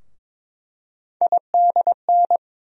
Morse code, heard as a keyed tone, about 22 wpm, 700 Hz. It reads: IDN
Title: USA Call Sign Postfixes - 3 Letters 22wpm